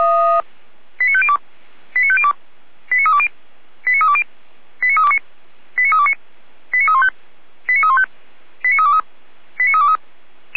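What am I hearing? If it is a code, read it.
Five-tone selective call (ZVEI), heard as quick runs of five tones, tones of about 70 ms. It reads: 89502 89502 89320 89320 89329 89329 89315 89315 89323 89323
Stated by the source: ZVEI five-tone selective call received with an SDR, from the audio server of the Orlamünde fire brigade (feuerwehr-orlamuende).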